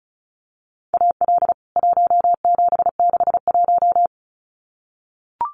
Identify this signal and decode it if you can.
Morse code, heard as a keyed tone, about 34 wpm, 700 Hz. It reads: AL 1761
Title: CWT Exchange 35wpm